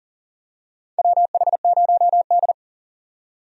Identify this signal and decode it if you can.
Morse code, heard as a keyed tone, about 40 wpm, 700 Hz. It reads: WH0D
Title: USA Call Signs - 4 Characters - 40wpm